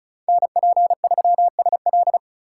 Morse code, heard as a keyed tone, about 36 wpm, 700 Hz. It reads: NP3SL
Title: USA Call Signs - 5 Characters - 2x Character Spacing - 35wpm